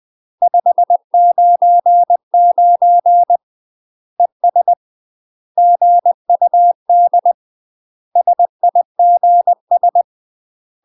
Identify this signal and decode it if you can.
Morse code, heard as a keyed tone, about 20 wpm, 700 Hz. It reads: 599 ES GUD SIGS
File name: UK MASTER QSOs TRAINING / 20WPM / UK MASTER QSOs 0001